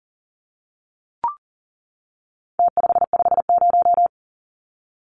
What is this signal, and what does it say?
Morse code, heard as a keyed tone, about 40 wpm, 700 Hz. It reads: T550
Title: Numbers - Contest Style - 4-Digit 40wpm